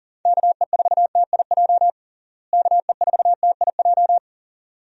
Morse code, heard as a keyed tone, about 37 wpm, 700 Hz. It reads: KE4TIJ KE4TIJ
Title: USA Call Signs - 6 Characters - Repeated 2x - 40wpm